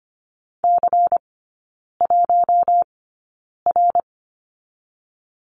Morse code, heard as a keyed tone, about 25 wpm, spaced as for 9 wpm, 700 Hz. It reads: C1R
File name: course / 25:10wpm / 093 Lesson 16e - Letter c - Three Characters 25wpm